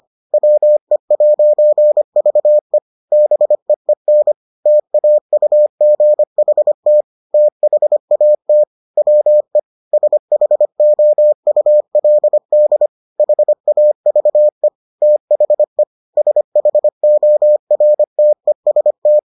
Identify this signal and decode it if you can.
Morse code, heard as a keyed tone, about 25 wpm, 600 Hz. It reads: WE'VE BEEN TAUGHT THAT WE SHOULD HAVE THE SHORTEST